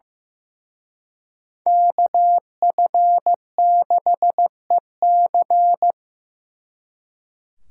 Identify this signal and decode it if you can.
Morse code, heard as a keyed tone, about 18 wpm, 700 Hz. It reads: KF6EC
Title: Speed Racing - USA Call Signs 15wpm